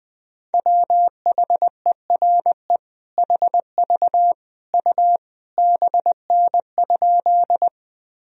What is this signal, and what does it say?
Morse code, heard as a keyed tone, about 20 wpm, 700 Hz. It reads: WHERE HV U BN?
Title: Sentences from Top 200 Words and Abbreviations 20wpm